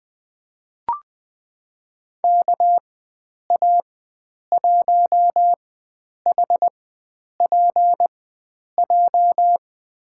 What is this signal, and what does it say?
Morse code, heard as a keyed tone, about 20 wpm, spaced as for 9 wpm, 700 Hz. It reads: KA1HPJ
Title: USA Call Signs - 6 Characters - 4x Character Spacing - 20wpm